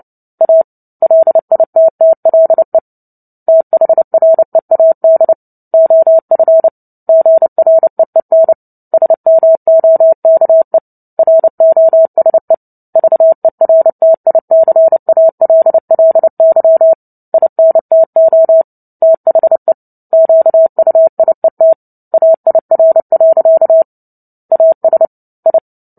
Morse code, heard as a keyed tone, about 30 wpm, 650 Hz. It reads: A LITTLE THREAD OF GREEN SMOKE ROSE VERTICALLY INTO THE QUIET AIR. AS I
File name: Book Two / 03-Chapter 3 - 29 WPM